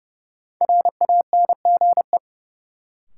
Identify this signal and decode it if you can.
Morse code, heard as a keyed tone, about 30 wpm, 700 Hz. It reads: RANGE
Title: Sets of 4 Words - New Word Drill - 30wpm